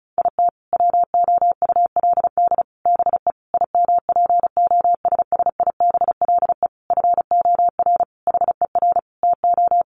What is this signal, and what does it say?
Morse code, heard as a keyed tone, about 34 wpm, 700 Hz. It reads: IT WOULD BE IMPOSSIBLE FOR HER TO
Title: Sentences from Top 1100 Words 35wpm